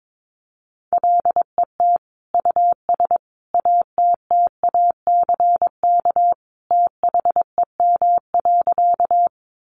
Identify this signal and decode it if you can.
Morse code, heard as a keyed tone, about 22 wpm, 700 Hz. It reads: LET US ATTACK THEM.
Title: Sentences from Top 1500 Words 22wpm